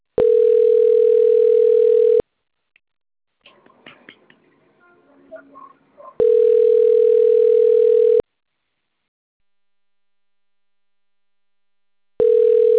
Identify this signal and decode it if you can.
Ringback tone (North America), 440 + 480 Hz.
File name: rg-1600-unknown- - - .72872